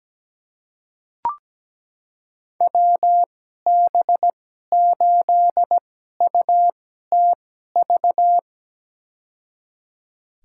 Morse code, heard as a keyed tone, about 17 wpm, spaced as for 12 wpm, 700 Hz. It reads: WB8UTV
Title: USA Call Signs - 6 Characters - 2x Character Spacing - 17wpm